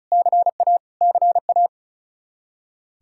Morse code, heard as a keyed tone, about 36 wpm, 700 Hz. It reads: CA CA
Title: Practice QSO - Exchange 1 - Beginning - 35wpm